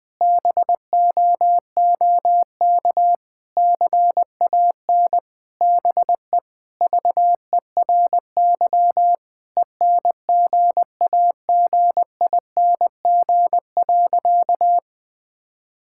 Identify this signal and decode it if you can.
Morse code, heard as a keyed tone, about 21 wpm, 700 Hz. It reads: BOOK CAN BE VERY ENGAGING.
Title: Sentences with 2-Word Phrases - 20wpm